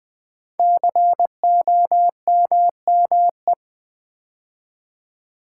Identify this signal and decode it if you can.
Morse code, heard as a keyed tone, about 20 wpm, 700 Hz. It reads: COMME
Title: Build A Word - 5 Letters At A Time - 10-Letter Words - 20wpm